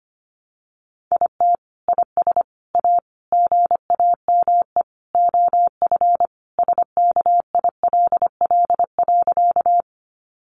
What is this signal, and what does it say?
Morse code, heard as a keyed tone, about 25 wpm, 700 Hz. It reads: IT IS A GAME OF SKILL.
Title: Sentences from Top 1200 Words 25wpm